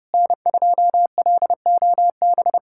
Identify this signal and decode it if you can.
Morse code, heard as a keyed tone, about 29 wpm, 700 Hz. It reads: N2LOB
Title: USA Callsign DE Callsign - 5 Characters - 30wpm